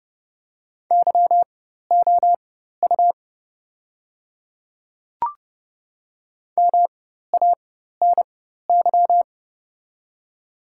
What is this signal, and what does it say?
Morse code, heard as a keyed tone, about 30 wpm, spaced as for 14 wpm, 700 Hz. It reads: YOU MANY
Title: 4x Character Spacing - Top 300 Words - 30wpm